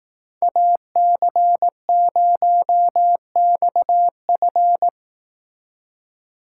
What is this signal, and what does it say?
Morse code, heard as a keyed tone, about 18 wpm, 700 Hz. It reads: AC0XF
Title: Speed Racing - USA 2x2 Call Signs 15wpm